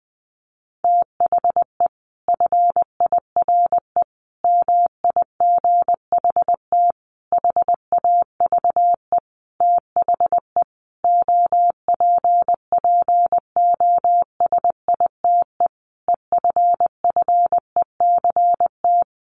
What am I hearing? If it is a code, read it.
Morse code, heard as a keyed tone, about 20 wpm, 700 Hz. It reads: THE FIRE MIGHT HAVE THE OPPOSITE EFFECT
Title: Sentences from Top 1300 Words 20wpm